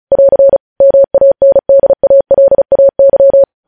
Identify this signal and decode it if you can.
Morse code, heard as a keyed tone, about 36 wpm, 550 Hz. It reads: <AR> MANDALAY
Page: news headlines in morse code at 35 wpm